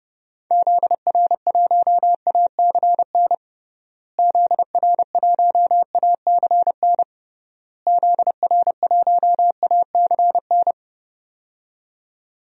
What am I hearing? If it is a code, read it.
Morse code, heard as a keyed tone, about 30 wpm, 700 Hz. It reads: ZR1ACN ZR1ACN ZR1ACN
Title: DXCC-100 Call Signs - 6 Characters - Repeated 3x - 30wpm